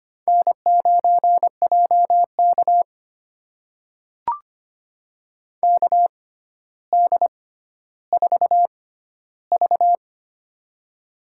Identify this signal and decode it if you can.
Morse code, heard as a keyed tone, about 25 wpm, spaced as for 13 wpm, 700 Hz. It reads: N9JK KD4V
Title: USA Call Signs - 4 Characters - 6x Character Spacing - 25wpm